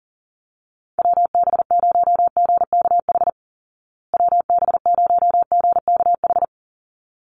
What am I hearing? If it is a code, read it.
Morse code, heard as a keyed tone, about 40 wpm, 700 Hz. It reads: WB0GKH WB0GKH
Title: Call Signs Repeated 3x - 40wpm